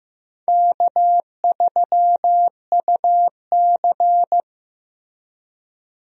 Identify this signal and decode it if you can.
Morse code, heard as a keyed tone, about 15 wpm, 700 Hz. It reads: K3UC
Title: USA Call Signs - 4 Characters - Repeated 2x - 15wpm